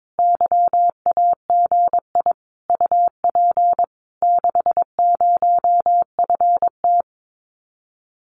Morse code, heard as a keyed tone, about 22 wpm, 700 Hz. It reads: YAGI UP 60FT
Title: Common QSO Phrases Encore 22wpm